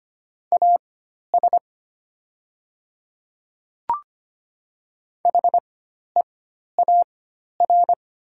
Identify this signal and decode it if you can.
Morse code, heard as a keyed tone, about 25 wpm, spaced as for 12 wpm, 700 Hz. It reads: AS HEAR
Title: 4x Character Spacing - Top 500 Words - 25wpm